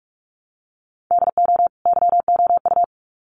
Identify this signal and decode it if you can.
Morse code, heard as a keyed tone, about 44 wpm, 700 Hz. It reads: DO YOU
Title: Sets of 2 Words 45wpm